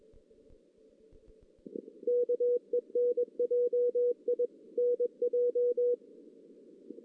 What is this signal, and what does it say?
Morse code, heard as a keyed tone, about 21 wpm, 490 Hz. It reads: KENJI NJ
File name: 2016 JAN CW NAQP / 2016 Jan naqp k5ot - 0-3